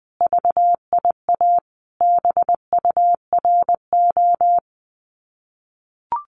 Morse code, heard as a keyed tone, about 20 wpm, 700 Hz. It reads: VIA BURO
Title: Common QSO Phrases Encore 20wpm